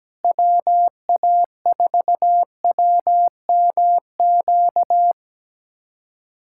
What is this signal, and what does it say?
Morse code, heard as a keyed tone, about 17 wpm, 700 Hz. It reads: WA4WMQ
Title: USA Callsign DE Callsign - 6 Characters - 17wpm